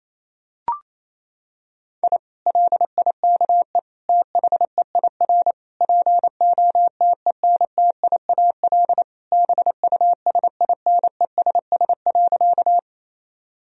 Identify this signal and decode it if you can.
Morse code, heard as a keyed tone, about 28 wpm, 700 Hz. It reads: I LIKE THEIR POTENTIAL BUSINESS.
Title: Sentences from Top 1000 Words - Review Part 4 - 28wpm